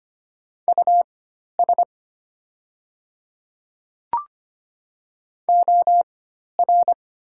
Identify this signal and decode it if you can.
Morse code, heard as a keyed tone, about 25 wpm, spaced as for 12 wpm, 700 Hz. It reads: US OR
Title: 4x Character Spacing - Top 200 Words - 25wpm